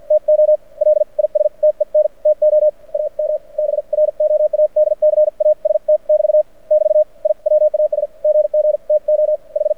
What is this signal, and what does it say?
Morse code, heard as a keyed tone, about 54 wpm, spaced as for 44 wpm, 610 Hz. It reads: TO LISTEN TO AM BROADCAST <BT> <BT> I WAS GG TO H
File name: archived / CRSnet